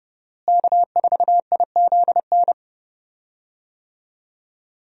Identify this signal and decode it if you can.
Morse code, heard as a keyed tone, about 30 wpm, 700 Hz. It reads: K4IZN
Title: USA Call Signs - 5 Characters - Repeated 2x - 30wpm